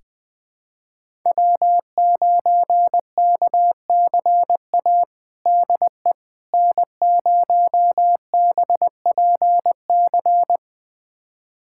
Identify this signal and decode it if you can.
Morse code, heard as a keyed tone, about 20 wpm, 700 Hz. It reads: W9KCA DE N0BPC
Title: USA Callsign DE Callsign - 5 Characters - 20wpm